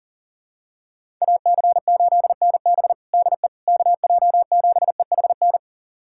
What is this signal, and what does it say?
Morse code, heard as a keyed tone, about 40 wpm, 700 Hz. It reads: AC8NB DE KJ7EHN
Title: USA Callsign DE Callsign 40wpm